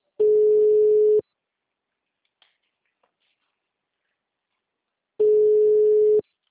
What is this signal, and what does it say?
Ringback tone (Europe, 425 Hz).
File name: out- -2005- - - .7